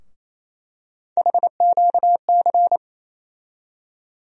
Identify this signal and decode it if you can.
Morse code, heard as a keyed tone, about 28 wpm, 700 Hz. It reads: HQC
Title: USA Call Sign Postfixes - 3 Letters 28wpm